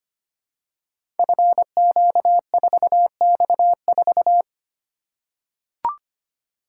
Morse code, heard as a keyed tone, about 25 wpm, 700 Hz. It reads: FQ4X4
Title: Code Groups - 5 Letter:Numbers - 25wpm